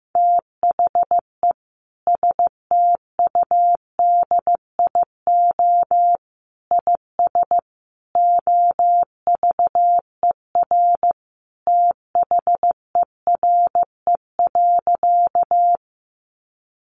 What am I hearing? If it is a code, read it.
Morse code, heard as a keyed tone, about 14 wpm, 700 Hz. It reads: THE STUDIO IS OVER THERE.
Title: Sentences from Top 1200 Words 15wpm